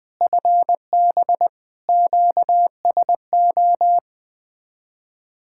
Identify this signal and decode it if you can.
Morse code, heard as a keyed tone, about 20 wpm, 700 Hz. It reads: FB QSO
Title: Speed-Racing Common QSO Phrases - 20wpm